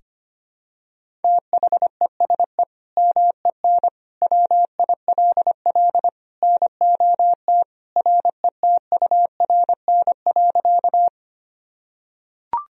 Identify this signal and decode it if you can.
Morse code, heard as a keyed tone, about 25 wpm, 700 Hz. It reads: THESE MEN WILL NOT RETURN.
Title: Sentences from Top 1000 Words - Review Part 1 - 25wpm